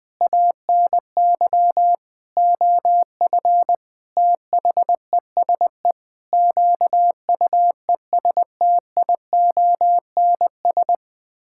Morse code, heard as a keyed tone, about 20 wpm, 700 Hz. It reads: ANY OF THESE QUESTIONS
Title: Sets of 9 Words - 20wpm